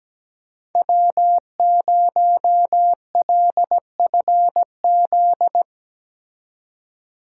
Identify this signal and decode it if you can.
Morse code, heard as a keyed tone, about 18 wpm, 700 Hz. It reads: W0LFZ
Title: Speed Racing - USA 1x3 Call Signs 17wpm